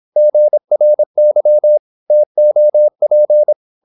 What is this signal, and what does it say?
Morse code, heard as a keyed tone, about 26 wpm, 600 Hz. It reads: GRY TOP
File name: Korte ord DK 26wpm